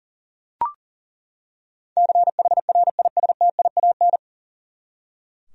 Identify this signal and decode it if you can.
Morse code, heard as a keyed tone, about 37 wpm, 700 Hz. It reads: CHRISTIAN
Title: USA Common Names - 40wpm